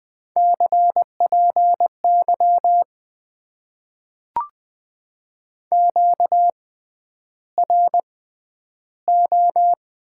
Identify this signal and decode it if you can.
Morse code, heard as a keyed tone, about 20 wpm, spaced as for 10 wpm, 700 Hz. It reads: CPY QRO
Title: Top 90 QSO Elements - 6x Character Spacing - 20wpm